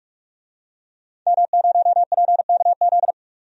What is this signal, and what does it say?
Morse code, heard as a keyed tone, about 42 wpm, 700 Hz. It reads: M0PKZ
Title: DXCC-100 Call Signs - 5 Characters - 45wpm